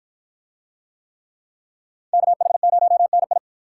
Morse code, heard as a keyed tone, about 53 wpm, 700 Hz. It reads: KH0NI